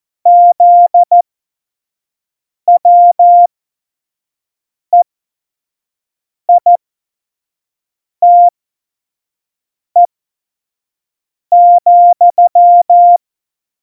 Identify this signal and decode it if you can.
Morse code, heard as a keyed tone, about 14 wpm, spaced as for 5 wpm, 700 Hz. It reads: ZWEITE,
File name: Morsekurs / Witze / Witz 13. November 2018, 25 F70